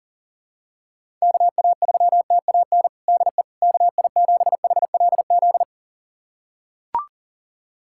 Morse code, heard as a keyed tone, about 37 wpm, 700 Hz. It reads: KA3TAN DE KI7HLZ
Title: USA Callsign DE Callsign - 6 Characters - 40wpm